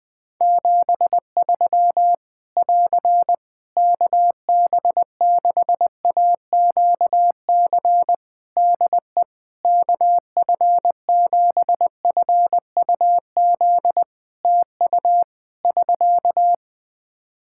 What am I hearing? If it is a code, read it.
Morse code, heard as a keyed tone, about 21 wpm, 700 Hz. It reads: 73 <AR> KB6AQC DE KF7FUZ TU <SK>
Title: Practice QSO - Exchange 4 - Ending - 20wpm